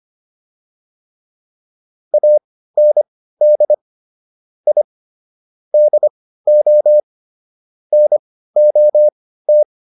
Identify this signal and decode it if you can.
Morse code, heard as a keyed tone, about 25 wpm, spaced as for 15 wpm, 600 Hz. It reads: AND I DO NOT